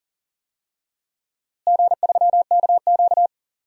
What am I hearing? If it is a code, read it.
Morse code, heard as a keyed tone, about 40 wpm, 700 Hz. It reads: G3KQ
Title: DXCC-100 Call Signs - 4 Characters - Repeated 2x - 40wpm